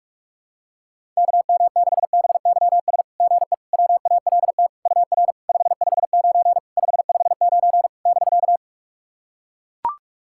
Morse code, heard as a keyed tone, about 44 wpm, 690 Hz. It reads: KM6BYS GE WALT UR 559 559 <BK>
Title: SOTA QSO Practice 45wpm